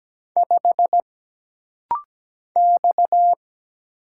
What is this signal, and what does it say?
Morse code, heard as a keyed tone, about 17 wpm, 700 Hz. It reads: 5 X
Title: Single Letter-Number Alt - Rapid-Fire 17wpm